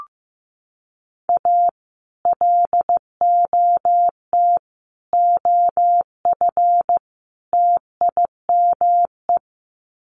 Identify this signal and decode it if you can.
Morse code, heard as a keyed tone, about 15 wpm, 700 Hz. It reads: A LOT OF TIME
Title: Sets of 4 Words 15wpm